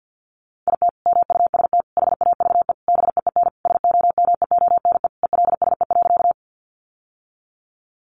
Morse code, heard as a keyed tone, about 50 wpm, 700 Hz. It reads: IT MUST HAVE BEEN SOMEONE ELSE.
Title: Sentences from Top 800 Words 50wpm